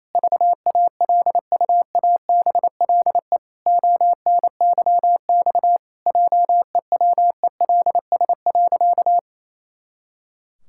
Morse code, heard as a keyed tone, about 28 wpm, 700 Hz. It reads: VALUABLE ONYX JEWELS.
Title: Pangram Challenge v2 - 28wpm